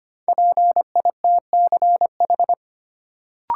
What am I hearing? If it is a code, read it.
Morse code, heard as a keyed tone, about 25 wpm, 700 Hz. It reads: PITCH
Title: Sentences from Top 1300 Words - New Word Drill - 25wpm